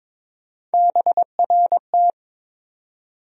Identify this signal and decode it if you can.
Morse code, heard as a keyed tone, about 21 wpm, 700 Hz. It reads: BRT
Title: USA Call Sign Postfixes - 3 Letters 22wpm